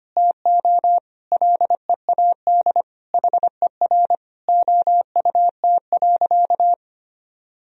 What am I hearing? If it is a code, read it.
Morse code, heard as a keyed tone, about 25 wpm, 700 Hz. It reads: TO LEAD HER OUT.
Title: Sentences from Top 1000 Words - Review Part 2 - 25wpm